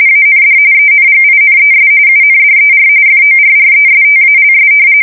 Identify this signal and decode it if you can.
RTTY, about 45.45 baud. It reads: AND ENGINES UNDER FORRESTALS
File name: fsk170 / battleshipnj